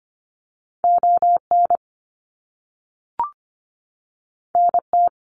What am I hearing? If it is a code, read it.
Morse code, heard as a keyed tone, about 25 wpm, 700 Hz. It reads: ON NT
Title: Canadian Provinces and Territories 25wpm